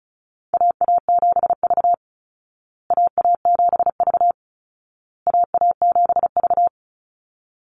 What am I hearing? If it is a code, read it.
Morse code, heard as a keyed tone, about 34 wpm, 700 Hz. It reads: AA7V AA7V AA7V
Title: Call Signs Repeated 2x - 35wpm